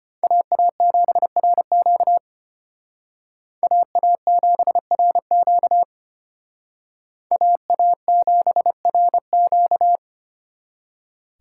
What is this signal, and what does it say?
Morse code, heard as a keyed tone, about 29 wpm, 700 Hz. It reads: AA7RQ AA7RQ AA7RQ
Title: Speed Racing - USA 2x2 Call Signs 25wpm